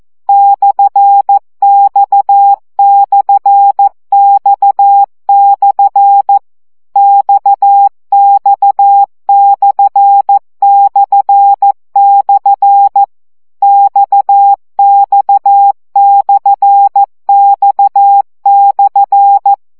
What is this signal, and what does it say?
Morse code, heard as a keyed tone, about 14 wpm, 800 Hz. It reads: /X/X/ XX/// XX/X/
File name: CW gyakorlas x per 12 baud